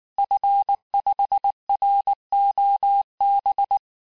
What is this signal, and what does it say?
Morse code, heard as a keyed tone, about 19 wpm, 800 Hz. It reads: F5ROB
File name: cours-du-lundi files / cours-1595--FIN-de-QSO-vitesse-18mpm